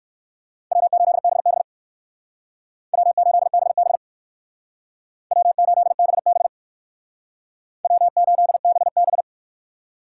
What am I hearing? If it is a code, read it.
Morse code, heard as a keyed tone, about 58 wpm, 700 Hz. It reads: W8BB W8BB W8BB W8BB